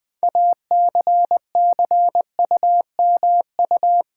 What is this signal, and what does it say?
Morse code, heard as a keyed tone, about 20 wpm, 700 Hz. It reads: ACCUMU
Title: Build A Word - 6 Letters At A Time - 12-Letter Words - 20wpm